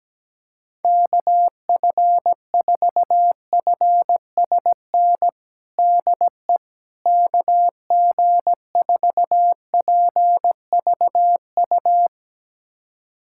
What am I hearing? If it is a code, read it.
Morse code, heard as a keyed tone, about 17 wpm, 700 Hz. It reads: KF4FSN DE KG4PVU
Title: USA Callsign DE Callsign - 6 Characters - 17wpm